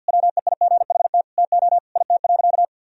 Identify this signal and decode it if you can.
Morse code, heard as a keyed tone, about 46 wpm, 700 Hz. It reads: WEIGHT TO IT.
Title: Sentences from Top 1000 Words - Review Part 3 - 50wpm